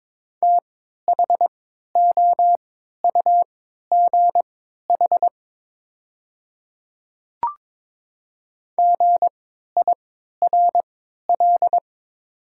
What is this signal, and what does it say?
Morse code, heard as a keyed tone, about 23 wpm, spaced as for 13 wpm, 690 Hz. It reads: THOUGH GIRL
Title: 3x Character Spacing - Top 500 Words - 22wpm